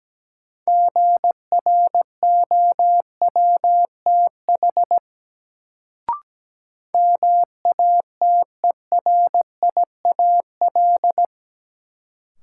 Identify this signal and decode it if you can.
Morse code, heard as a keyed tone, about 17 wpm, 700 Hz. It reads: GROWTH MATERIAL
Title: Sentences from Top 1000 Words - Review Part 5 - New Word Drill - 17wpm